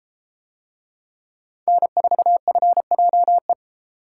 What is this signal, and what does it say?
Morse code, heard as a keyed tone, about 33 wpm, 700 Hz. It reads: N4FJE